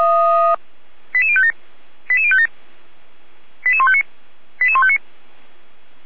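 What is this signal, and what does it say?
Five-tone selective call (ZVEI), heard as quick runs of five tones, tones of about 70 ms. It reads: 88057 88057 88159 88159
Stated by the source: ZVEI five-tone selective call received on an SDR.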